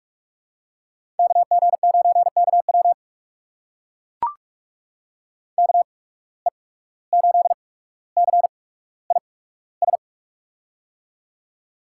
Morse code, heard as a keyed tone, about 44 wpm, spaced as for 18 wpm, 700 Hz. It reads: KG9KW KE8CIS
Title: USA Call Signs - 8x Character Spacing - 45wpm